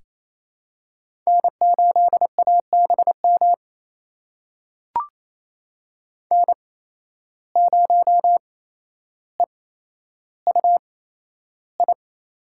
Morse code, heard as a keyed tone, about 28 wpm, spaced as for 12 wpm, 700 Hz. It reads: N8ABM N0EUI
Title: USA Call Signs - 5 Characters - 8x Character Spacing - 28wpm